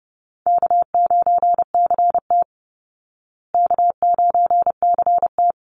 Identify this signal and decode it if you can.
Morse code, heard as a keyed tone, about 30 wpm, 700 Hz. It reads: K9CT K9CT
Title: Easy Call Signs Repeated 2x - 30wpm